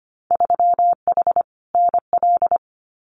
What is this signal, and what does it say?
Morse code, heard as a keyed tone, about 25 wpm, 700 Hz. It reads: 3H NL
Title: Winter Field Day Exchanges Encore 25wpm